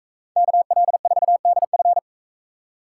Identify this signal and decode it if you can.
Morse code, heard as a keyed tone, about 42 wpm, 700 Hz. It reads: KL4DF